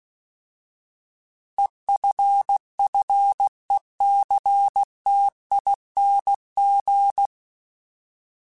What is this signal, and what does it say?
Morse code, heard as a keyed tone, about 16 wpm, 800 Hz. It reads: EFFECTING
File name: Alphabetical / Alphabetical mp3s / Es @15